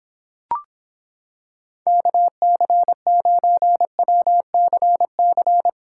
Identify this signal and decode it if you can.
Morse code, heard as a keyed tone, about 26 wpm, 700 Hz. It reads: KC9WCC